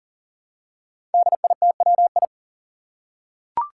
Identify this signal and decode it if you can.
Morse code, heard as a keyed tone, about 41 wpm, 700 Hz. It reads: DITWI
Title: Five Challenging Letter-Number Combinations 40wpm